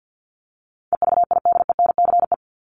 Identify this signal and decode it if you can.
Morse code, heard as a keyed tone, about 50 wpm, 700 Hz. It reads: EVIDENCE
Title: Top 1001-1250 Words 50wpm